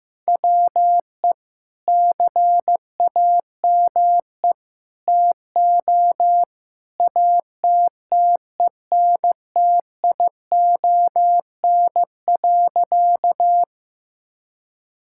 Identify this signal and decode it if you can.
Morse code, heard as a keyed tone, about 15 wpm, 700 Hz. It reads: WE CAME TO ATTENTION.
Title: Sentences from Top 1000 Words - Review Part 6 - 15wpm